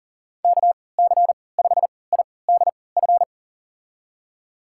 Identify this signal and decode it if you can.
Morse code, heard as a keyed tone, about 40 wpm, spaced as for 23 wpm, 700 Hz. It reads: KC5IDF
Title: USA Call Signs - 6 Characters - 3x Character Spacing - 40wpm